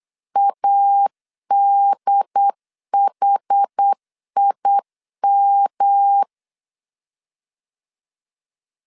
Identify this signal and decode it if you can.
Morse code, heard as a keyed tone, about 8 wpm, 800 Hz. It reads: ADHIM
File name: CW / DET / 1 3 1500C S B 8wpm 4141